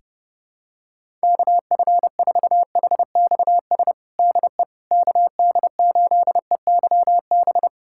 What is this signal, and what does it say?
Morse code, heard as a keyed tone, about 32 wpm, 700 Hz. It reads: KF4HXS DE KD8EYB
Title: USA Callsign DE Callsign - 6 Characters - 30wpm